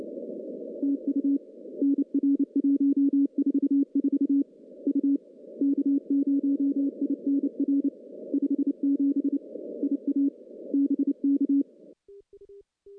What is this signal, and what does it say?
Morse code, heard as a keyed tone, about 29 wpm, 280 Hz. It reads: TU NR 144 U K0INR 57 IA BK
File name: contests / CW-SS-k5tr- -0600